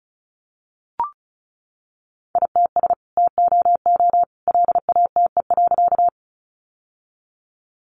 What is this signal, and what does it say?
Morse code, heard as a keyed tone, about 34 wpm, 700 Hz. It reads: ITS TOO LATE.
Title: Sentences from Top 600 Words 35wpm